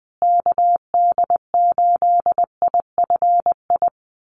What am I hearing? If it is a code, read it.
Morse code, heard as a keyed tone, about 20 wpm, 700 Hz. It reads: KD8IFI
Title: USA Call Signs Repeated 2x 20wpm